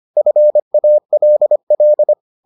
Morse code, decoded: FALL

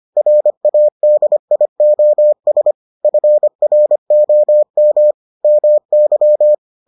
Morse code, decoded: RADIOS FROM MY

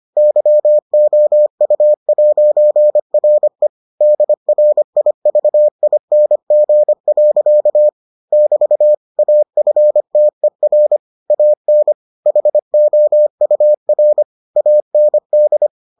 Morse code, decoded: YOU'RE DRIVING. <BT> AFTER AN HOUR AND

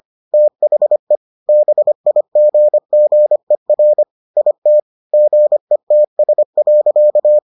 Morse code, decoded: THE BIGGER IT GETS.

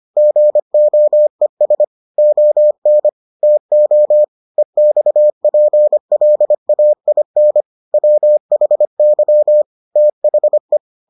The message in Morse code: GOES ON TO EXPLAIN WHY THE

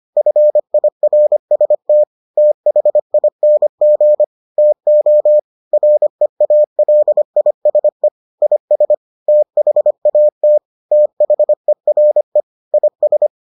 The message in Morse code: FIRST THING TO REALISE IS THAT THERE IS

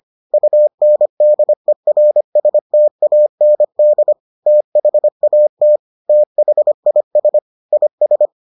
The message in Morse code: UNDERSTAND THAT THIS IS

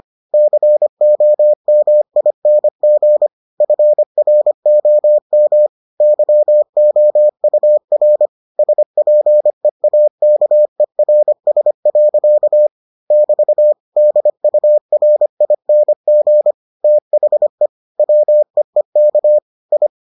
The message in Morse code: COMING FROM YOUR SPEAKERS. <BT> DURING THE WEEK I

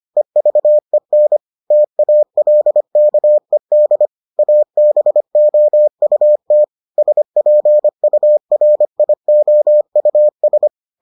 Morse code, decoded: EVEN TALKED ABOUT SPURIOUS

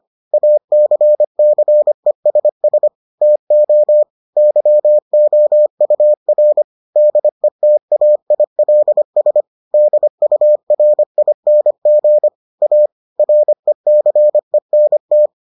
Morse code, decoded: ACCESS TO YOUR DETAILS DURING A RECENT

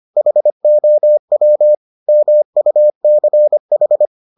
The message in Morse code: HOW MUCH

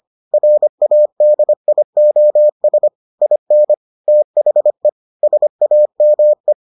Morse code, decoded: RADIOS IN THE SAME

